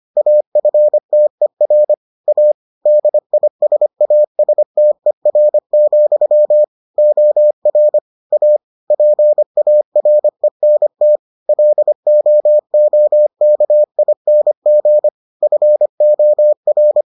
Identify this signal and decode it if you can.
Morse code, heard as a keyed tone, about 25 wpm, 600 Hz. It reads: AFTER A DISASTER, OR A PARENT LOOKING FOR